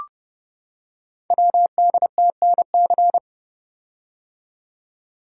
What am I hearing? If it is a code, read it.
Morse code, heard as a keyed tone, about 30 wpm, 700 Hz. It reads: WDTNC
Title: Code Groups - 5 Letters - 30wpm